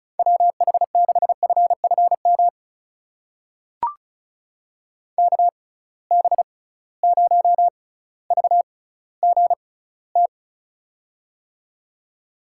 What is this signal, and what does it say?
Morse code, heard as a keyed tone, about 36 wpm, spaced as for 18 wpm, 700 Hz. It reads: WH6FFM KB0VGT
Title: USA Call Signs - 6 Characters - 6x Character Spacing - 35wpm